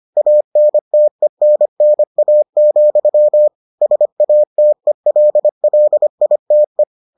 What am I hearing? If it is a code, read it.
Morse code, heard as a keyed tone, about 25 wpm, 600 Hz. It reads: ANTENNA, SATELLITE